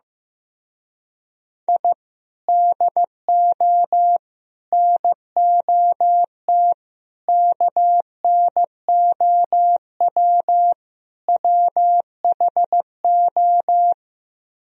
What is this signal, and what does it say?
Morse code, heard as a keyed tone, about 15 wpm, 700 Hz. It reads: I DO NOT KNOW WHO
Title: Speed-Racing - Sets of 5 Words 15wpm